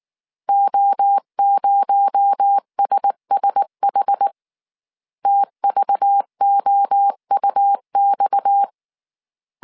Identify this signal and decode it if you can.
Morse code, heard as a keyed tone, about 19 wpm, 800 Hz. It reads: O0SSH TVOUX
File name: CW / DET / 1 6 1500C S C 18wpm 2356